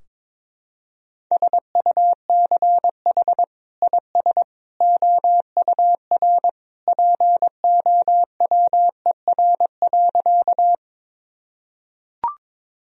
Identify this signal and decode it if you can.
Morse code, heard as a keyed tone, about 21 wpm, 700 Hz. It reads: SUCH IS OUR POWER.
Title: Sentences from Top 1000 Words - Review Part 2 - 22wpm